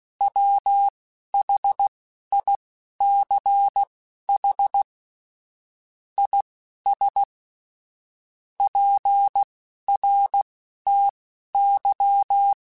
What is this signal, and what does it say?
Morse code, heard as a keyed tone, about 16 wpm, spaced as for 11 wpm, 800 Hz. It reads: WHICH IS PRTY
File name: QSOs / QSOs-mp3s / WA7QOX 1@10